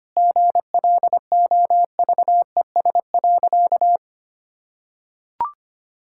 Morse code, heard as a keyed tone, about 25 wpm, 700 Hz. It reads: GLOVES.
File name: Easy Sentences with Hints - 25wpm